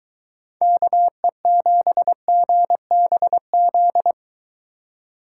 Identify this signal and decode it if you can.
Morse code, heard as a keyed tone, about 22 wpm, 700 Hz. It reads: KE7GBZ